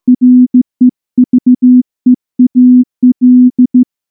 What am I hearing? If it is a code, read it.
Morse code, heard as a keyed tone, about 14 wpm, 250 Hz. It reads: REVEAL
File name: morse-training-15WPM